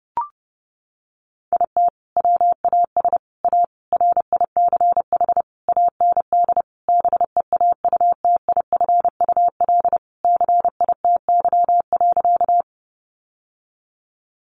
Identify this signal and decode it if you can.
Morse code, heard as a keyed tone, about 30 wpm, 700 Hz. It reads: IT WAS A RICH AND BEAUTIFUL CITY.
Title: Sentences from Top 1100 Words 30wpm